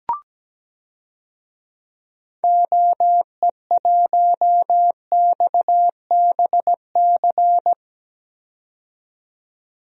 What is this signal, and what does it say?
Morse code, heard as a keyed tone, about 17 wpm, 700 Hz. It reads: OE1XBC
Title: DXCC-100 Call Signs - 6 Characters - 17wpm